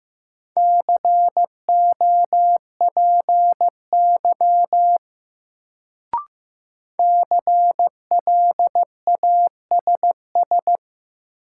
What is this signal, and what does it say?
Morse code, heard as a keyed tone, about 15 wpm, 700 Hz. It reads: COPY CLASS
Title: Sentences from Top 1400 Words - New Word Drill Alt - 15wpm